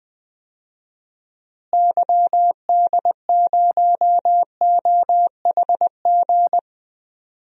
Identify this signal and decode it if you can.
Morse code, heard as a keyed tone, about 20 wpm, 700 Hz. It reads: YD0OHG
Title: DXCC-100 Call Signs - 6 Characters - Repeated 2x - 20wpm